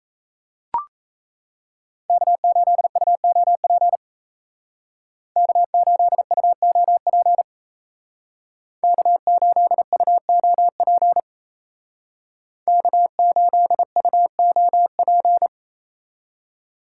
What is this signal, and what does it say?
Morse code, heard as a keyed tone, about 36 wpm, 700 Hz. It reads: K8UOP K8UOP K8UOP K8UOP